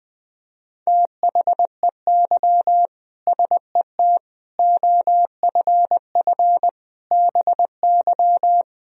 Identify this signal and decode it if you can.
Morse code, heard as a keyed tone, about 20 wpm, 700 Hz. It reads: THEY SET OFF BY
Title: Sentences from Top 1000 Words - Review Part 3 - 20wpm